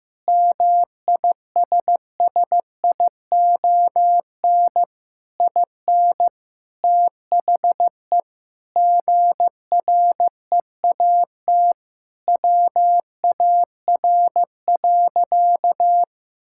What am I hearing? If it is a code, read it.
Morse code, heard as a keyed tone, about 15 wpm, 700 Hz. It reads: MISSION IN THE GREAT WAR.
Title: Sentences from Top 1600 Words 15wpm